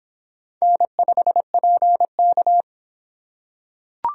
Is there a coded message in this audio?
Yes: Morse code from 0.6 to 2.6 s, heard as a keyed tone, about 26 wpm, 700 Hz. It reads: N5PK